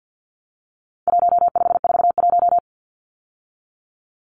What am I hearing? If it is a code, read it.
Morse code, heard as a keyed tone, about 50 wpm, 700 Hz. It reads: 1541
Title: Numbers - 4-Digit 50wpm